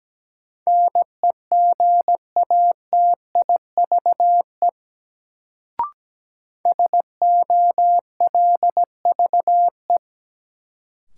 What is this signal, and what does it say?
Morse code, heard as a keyed tone, about 17 wpm, 700 Hz. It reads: NEGATIVE SOLVE
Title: Sentences from Top 1000 Words - New Word Drill - 17wpm